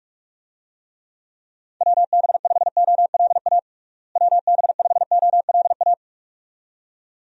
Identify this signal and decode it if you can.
Morse code, heard as a keyed tone, about 46 wpm, 700 Hz. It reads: WB5OLA WB5OLA
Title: USA Call Signs - 6 Characters - Repeated 2x - 45wpm